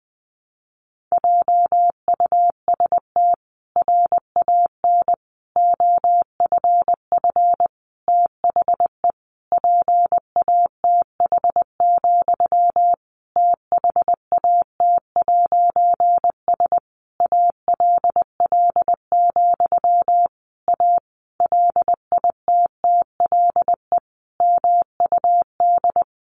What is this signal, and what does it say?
Morse code, heard as a keyed tone, about 20 wpm, 700 Hz. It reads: JUST RAN OFF THE PATH, THAT'S ALL, A LITTLE MUD
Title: A Sound Of Thunder 20wpm